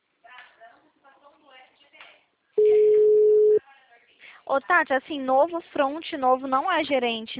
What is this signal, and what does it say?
Ringback tone (Europe, 425 Hz), with speech.